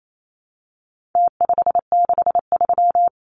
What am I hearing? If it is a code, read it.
Morse code, heard as a keyed tone, about 28 wpm, 700 Hz. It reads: T563
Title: Numbers - Contest Style - 4-Digit 28wpm